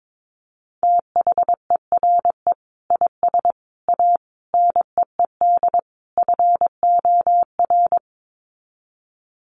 Morse code, heard as a keyed tone, about 22 wpm, 700 Hz. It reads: THERE IS A NEED FOR
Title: Sets of 5 Words Encore2 22wpm